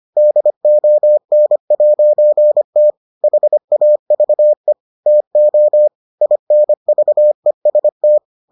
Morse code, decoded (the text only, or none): DON'T HAVE TO INVEST